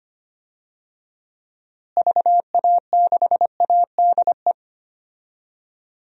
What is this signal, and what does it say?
Morse code, heard as a keyed tone, about 25 wpm, 700 Hz. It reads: VA6ADE
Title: DXCC-100 Call Signs - 6 Characters - Repeated 2x - 25wpm